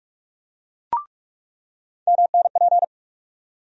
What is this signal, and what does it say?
Morse code, heard as a keyed tone, about 46 wpm, 700 Hz. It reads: MNP
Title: USA Call Sign Postfixes - 3 Letters 45wpm